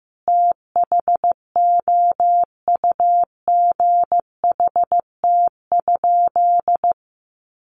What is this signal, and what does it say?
Morse code, heard as a keyed tone, about 15 wpm, 700 Hz. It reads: THOUGHT?
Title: Sentences from Top 300 Words 15wpm